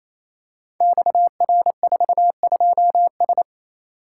Morse code, heard as a keyed tone, about 28 wpm, 700 Hz. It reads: XR42S
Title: Code Groups - 5 Letter:Numbers - 28wpm